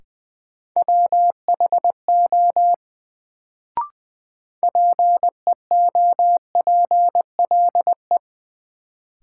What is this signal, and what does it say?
Morse code, heard as a keyed tone, about 20 wpm, 700 Hz. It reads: WHO PEOPLE
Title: Sets of 2 Words - New Word Drill - 20wpm